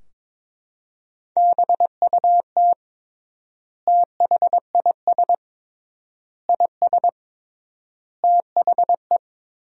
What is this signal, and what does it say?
Morse code, heard as a keyed tone, about 22 wpm, 700 Hz. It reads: BUT THIS IS THE
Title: 3x Word-Spacing - Sentences from Top 200 Words - 22wpm